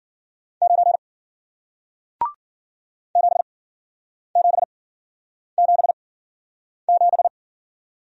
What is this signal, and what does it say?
Morse code, heard as a keyed tone, about 53 wpm, 700 Hz. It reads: 9 7 7 7 7